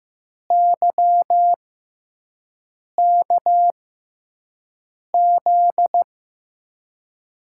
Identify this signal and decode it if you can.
Morse code, heard as a keyed tone, about 15 wpm, spaced as for 5 wpm, 700 Hz. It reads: YKZ
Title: USA Call Signs - 6x Character Spacing - 15wpm